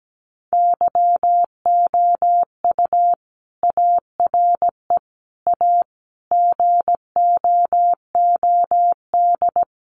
Morse code, heard as a keyed tone, about 17 wpm, 700 Hz. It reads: YOU ARE A GOOD
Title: Sentences from Top 200 Words 17wpm